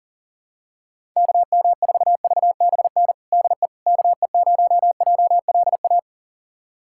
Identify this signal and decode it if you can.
Morse code, heard as a keyed tone, about 40 wpm, 700 Hz. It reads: KM4VBN DE KE0JLA
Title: USA Callsign DE Callsign - 6 Characters - 40wpm